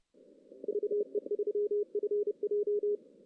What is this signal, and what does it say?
Morse code, heard as a keyed tone, about 29 wpm, 430 Hz. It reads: VE3FJ
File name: 2016 JAN CW NAQP / 2016 Jan naqp k5ot - 0-5